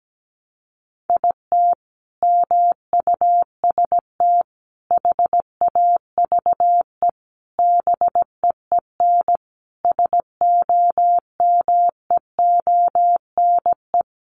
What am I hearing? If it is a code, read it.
Morse code, heard as a keyed tone, about 17 wpm, 700 Hz. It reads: IT MUST HAVE BEEN SOMEONE
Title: Sentences from Top 800 Words 17wpm